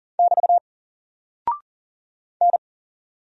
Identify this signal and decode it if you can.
Morse code, heard as a keyed tone, about 40 wpm, 700 Hz. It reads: <BT> N
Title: Single Character (Letter:Number:Punctuation:Prosign) 40wpm